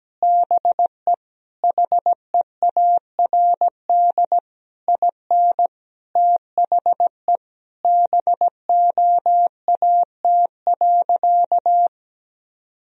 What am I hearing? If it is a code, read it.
Morse code, heard as a keyed tone, about 17 wpm, 700 Hz. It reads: BE HEARD IN THE BOAT.
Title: Sentences from Top 1000 Words - Review Part 3 - 17wpm